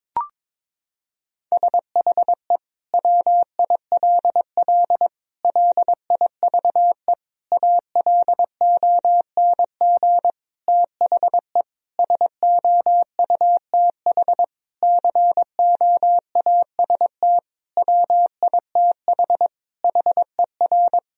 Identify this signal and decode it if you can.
Morse code, heard as a keyed tone, about 21 wpm, 700 Hz. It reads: SHE WILL LIVE ALONG THE SOUTH COAST WITH HER
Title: Sentences from Top 1000 Words - Review Part 4 - 22wpm